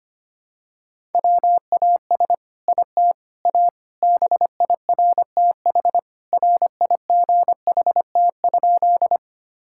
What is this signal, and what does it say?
Morse code, heard as a keyed tone, about 25 wpm, 700 Hz. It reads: WAS IT A BIRTH RIGHT?
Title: Sentences from Top 1000 Words - Review Part 5 - 25wpm